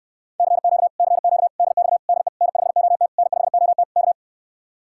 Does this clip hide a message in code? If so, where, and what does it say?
Yes: Morse code from 0.4 to 4.1 s, heard as a keyed tone, about 69 wpm, 700 Hz. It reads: CQ CQ DX DE N5OT N5OT K